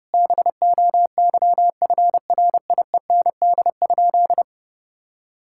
Morse code, heard as a keyed tone, about 30 wpm, 700 Hz. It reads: BOYFRIEND?
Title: Sentences from Top 1000 Words - Review Part 5 - 30wpm